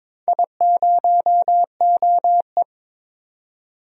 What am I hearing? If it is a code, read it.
Morse code, heard as a keyed tone, about 22 wpm, 700 Hz. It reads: I0OE
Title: DXCC-100 Call Signs - 4 Characters - 22wpm